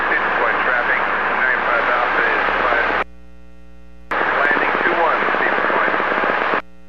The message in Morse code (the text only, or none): none